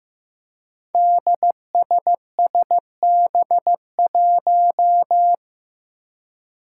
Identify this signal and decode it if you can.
Morse code, heard as a keyed tone, about 15 wpm, 700 Hz. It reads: DSSB1
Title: Five Challenging Letter-Number Combinations 15wpm